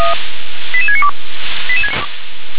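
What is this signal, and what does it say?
Five-tone selective call (ZVEI), heard as quick runs of five tones, tones of about 70 ms. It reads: 88692
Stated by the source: ZVEI five-tone call 88788 received on SDR.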